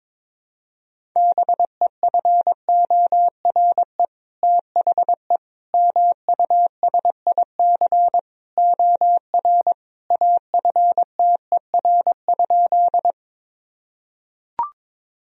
Morse code, heard as a keyed tone, about 22 wpm, 700 Hz. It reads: BEFORE THE MUSIC OR AFTER?
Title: Sentences from Top 1000 Words - Review Part 1 - 22wpm